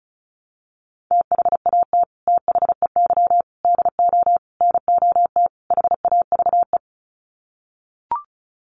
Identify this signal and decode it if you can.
Morse code, heard as a keyed tone, about 34 wpm, 700 Hz. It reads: THAT THEY DO NOT HAVE
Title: Sets of 5 Words Encore2 35wpm